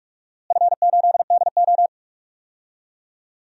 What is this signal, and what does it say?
Morse code, heard as a keyed tone, about 43 wpm, 690 Hz. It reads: F8DO
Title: DXCC-100 Call Signs - 4 Characters - Repeated 3x - 45wpm